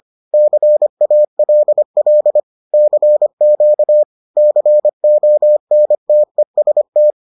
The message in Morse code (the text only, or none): CALL CQ CONTEST